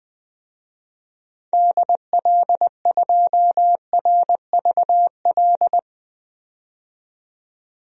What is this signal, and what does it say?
Morse code, heard as a keyed tone, about 21 wpm, 700 Hz. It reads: DL2RVL
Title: DXCC-100 Call Signs - 6 Characters - 20wpm